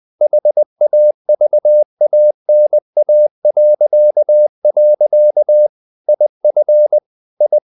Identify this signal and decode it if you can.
Morse code, heard as a keyed tone, about 20 wpm, 600 Hz. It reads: HAVANA.. IF I
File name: the old man and the sea part 2 20wpm